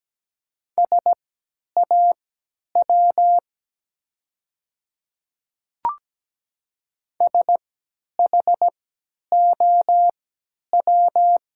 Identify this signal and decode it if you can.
Morse code, heard as a keyed tone, about 17 wpm, spaced as for 10 wpm, 690 Hz. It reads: SAW SHOW
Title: 3x Character Spacing - Top 300 Words - 17wpm